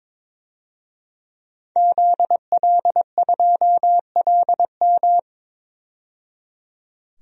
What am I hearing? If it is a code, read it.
Morse code, heard as a keyed tone, about 22 wpm, 700 Hz. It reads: ZL2LM
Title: DXCC-100 Call Signs - 5 Characters - 22wpm